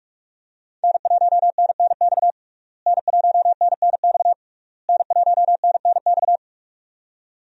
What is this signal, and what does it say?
Morse code, heard as a keyed tone, about 46 wpm, 700 Hz. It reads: N1NNX N1NNX N1NNX
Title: USA Call Signs - 5 Characters - Repeated 3x - 45wpm